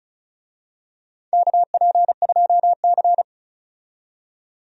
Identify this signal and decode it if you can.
Morse code, heard as a keyed tone, about 34 wpm, 700 Hz. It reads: KP2C
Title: DXCC-100 Call Signs - 4 Characters - 35wpm